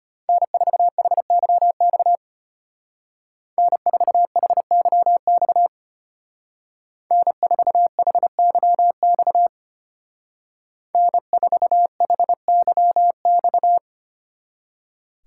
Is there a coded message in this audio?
Yes: Morse code from 0.3 to 13.8 s, heard as a keyed tone, about 32 wpm, 700 Hz. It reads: N4HYX N4HYX N4HYX N4HYX